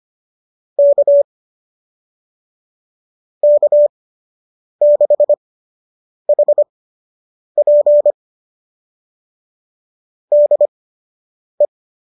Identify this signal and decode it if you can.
Morse code, heard as a keyed tone, about 25 wpm, spaced as for 8 wpm, 590 Hz. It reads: K K6HP DE
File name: qso003 08